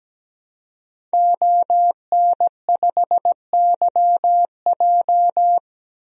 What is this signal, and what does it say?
Morse code, heard as a keyed tone, about 17 wpm, 700 Hz. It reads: ON5YJ
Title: DXCC-100 Call Signs - 5 Characters - Repeated 3x - 17wpm